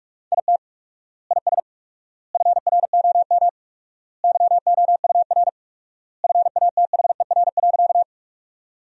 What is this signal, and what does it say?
Morse code, heard as a keyed tone, about 48 wpm, 700 Hz. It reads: IT IS FROM YOUR FATHER.
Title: 4x Word-Spacing - Sentences from Top 300 Words - 45wpm